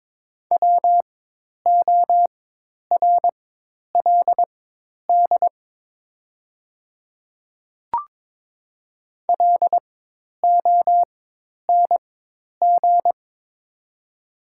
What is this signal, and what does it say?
Morse code, heard as a keyed tone, about 22 wpm, spaced as for 10 wpm, 700 Hz. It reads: WORLD LONG
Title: 4x Character Spacing - Top 300 Words - 22wpm